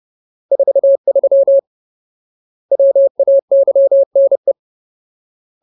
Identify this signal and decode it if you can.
Morse code, heard as a keyed tone, about 30 wpm, 550 Hz. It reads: 43 WAYNE
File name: Ultimate Word List / 30WPM Ultimate